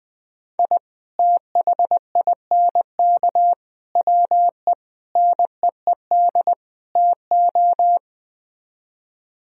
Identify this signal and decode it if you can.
Morse code, heard as a keyed tone, about 21 wpm, 700 Hz. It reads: I THINK WE NEED TO